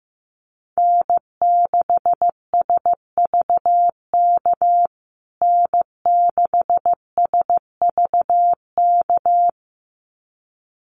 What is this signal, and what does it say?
Morse code, heard as a keyed tone, about 15 wpm, 700 Hz. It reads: N6SVK N6SVK
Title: USA Call Signs Repeated 2x 15wpm